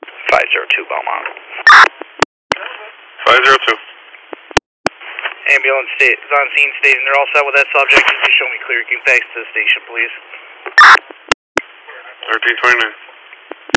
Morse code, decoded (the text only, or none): none